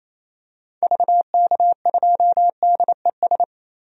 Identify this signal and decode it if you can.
Morse code, heard as a keyed tone, about 27 wpm, 700 Hz. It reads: VK2DES
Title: DXCC-100 Call Signs - 6 Characters - Repeated 2x - 28wpm